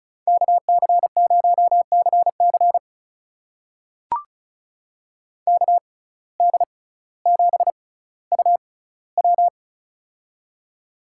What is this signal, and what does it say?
Morse code, heard as a keyed tone, about 36 wpm, spaced as for 18 wpm, 700 Hz. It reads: KC0CC KD7UW
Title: USA Call Signs - 5 Characters - 6x Character Spacing - 35wpm